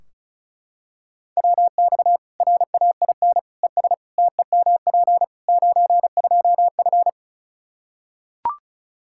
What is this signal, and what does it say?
Morse code, heard as a keyed tone, about 34 wpm, 700 Hz. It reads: WX RAIN ES TEMP 92F
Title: Practice QSO - Exchange 2 - Info - 35wpm